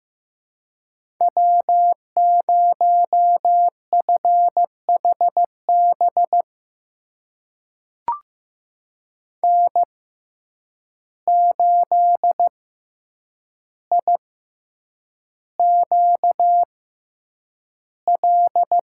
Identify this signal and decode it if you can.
Morse code, heard as a keyed tone, about 15 wpm, spaced as for 8 wpm, 700 Hz. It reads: W0FHB N8IQL
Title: USA Call Signs - 5 Characters - 6x Character Spacing - 15wpm